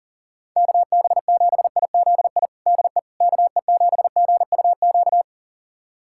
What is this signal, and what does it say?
Morse code, heard as a keyed tone, about 40 wpm, 700 Hz. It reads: KB7IZI DE KE7GUQ
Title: USA Callsign DE Callsign - 6 Characters - 40wpm